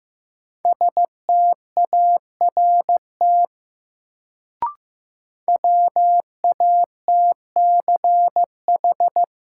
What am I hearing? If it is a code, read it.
Morse code, heard as a keyed tone, about 15 wpm, 700 Hz. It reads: START WATCH
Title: Sentences from Top 300 Words - New Word Drill - 15wpm